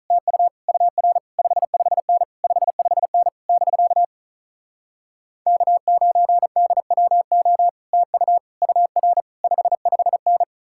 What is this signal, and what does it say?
Morse code, heard as a keyed tone, about 47 wpm, 700 Hz. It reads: TU UR 55N 55N <BK> K9DWO TU UR 55N